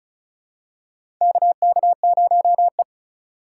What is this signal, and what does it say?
Morse code, heard as a keyed tone, about 34 wpm, 700 Hz. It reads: KK0E
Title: USA Call Signs - 3x Character Spacing - 35wpm